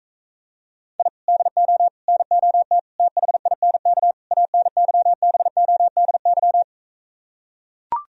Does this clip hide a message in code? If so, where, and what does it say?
Yes: Morse code from 1.0 to 6.6 s, heard as a keyed tone, about 46 wpm, 700 Hz. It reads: I DO NOT THINK ANYBODY